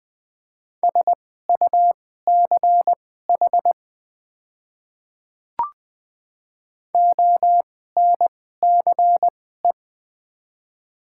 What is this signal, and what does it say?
Morse code, heard as a keyed tone, about 20 wpm, spaced as for 14 wpm, 700 Hz. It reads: SUCH ONCE
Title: 2x Character Spacing - Top 300 Words - 20wpm